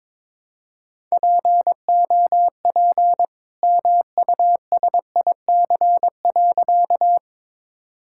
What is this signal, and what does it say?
Morse code, heard as a keyed tone, about 22 wpm, 700 Hz. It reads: POP MUSIC.
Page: morse-code-ninja-practice-22wpm